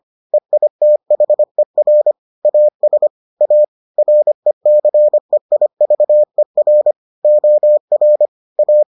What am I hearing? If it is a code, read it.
Morse code, heard as a keyed tone, about 25 wpm, 600 Hz. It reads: EITHER AS A RECEIVER OR A